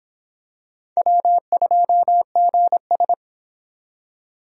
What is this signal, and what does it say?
Morse code, heard as a keyed tone, about 25 wpm, 700 Hz. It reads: W2GS